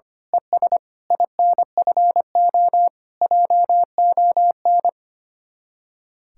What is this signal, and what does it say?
Morse code, heard as a keyed tone, about 24 wpm, 700 Hz. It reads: ES INFO JON
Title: Speed-Racing Common QSO Phrases - 25wpm